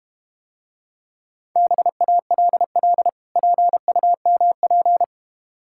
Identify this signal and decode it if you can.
Morse code, heard as a keyed tone, about 32 wpm, 700 Hz. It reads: BALL PUMP